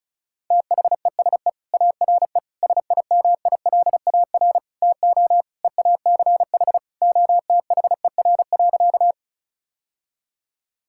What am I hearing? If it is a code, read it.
Morse code, heard as a keyed tone, about 34 wpm, 700 Hz. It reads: THESE ARE SIMILAR TO EACH OTHER.
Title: Sentences from Top 1000 Words - Review Part 5 - 35wpm